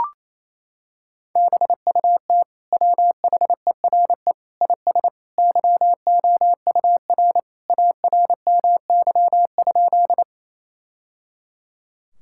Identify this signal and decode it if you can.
Morse code, heard as a keyed tone, about 27 wpm, 700 Hz. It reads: BUT WHERE IS YOUR ARMY?
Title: Sentences from Top 1000 Words - Review Part 5 - 28wpm